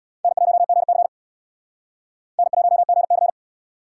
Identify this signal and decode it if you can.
Morse code, heard as a keyed tone, about 62 wpm, 700 Hz. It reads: N1GC N1GC
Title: Speed Racing - USA 1x2 Call Signs 50wpm